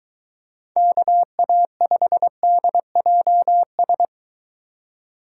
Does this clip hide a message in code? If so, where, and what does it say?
Yes: Morse code from 0.8 to 4.1 s, heard as a keyed tone, about 23 wpm, 700 Hz. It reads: KA5DJS